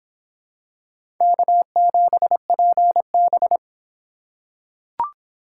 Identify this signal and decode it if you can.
Morse code, heard as a keyed tone, about 25 wpm, 700 Hz. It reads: K7PB